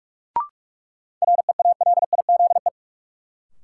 Morse code, heard as a keyed tone, about 44 wpm, 700 Hz. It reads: REALIZE
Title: Sentences from Top 1000 Words - New Word Drill - 45wpm